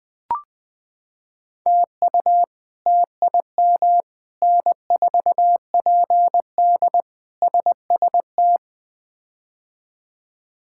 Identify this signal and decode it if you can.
Morse code, heard as a keyed tone, about 21 wpm, 700 Hz. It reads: TU TIM N4PD SST
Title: K1USN SST QSO Practice - 20wpm